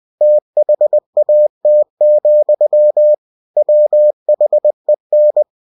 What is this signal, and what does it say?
Morse code, heard as a keyed tone, about 20 wpm, 600 Hz. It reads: THAT, WHEN
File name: cw-books / winnie-the-pooh / winnie-20wpm-page-0018